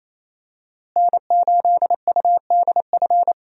Morse code, heard as a keyed tone, about 28 wpm, 700 Hz. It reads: N8UDF
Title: USA Call Signs - 5 Characters - Repeated 2x - 28wpm